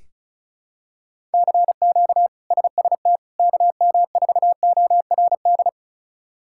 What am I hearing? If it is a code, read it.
Morse code, heard as a keyed tone, about 36 wpm, 700 Hz. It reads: CQ SST KM4ORD
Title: K1USN SST QSO Practice - 35wpm